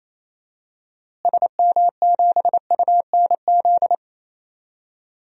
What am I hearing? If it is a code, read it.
Morse code, heard as a keyed tone, about 28 wpm, 700 Hz. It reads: SM7UNZ
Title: DXCC-100 Call Signs - 6 Characters - Repeated 2x - 28wpm